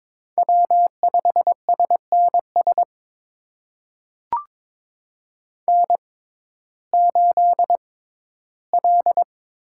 Morse code, heard as a keyed tone, about 22 wpm, spaced as for 13 wpm, 700 Hz. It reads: W5SNS N8L
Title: USA Call Signs - 5 Characters - 6x Character Spacing - 22wpm